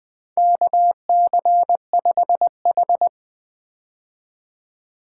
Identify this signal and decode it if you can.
Morse code, heard as a keyed tone, about 20 wpm, 700 Hz. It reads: KC5H
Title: 2023 POTA Hunter Call Signs - 20wpm